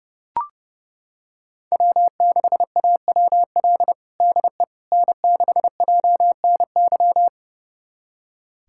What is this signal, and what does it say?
Morse code, heard as a keyed tone, about 30 wpm, 700 Hz. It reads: W6AWL DE N6JNY
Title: USA Callsign DE Callsign - 5 Characters - 30wpm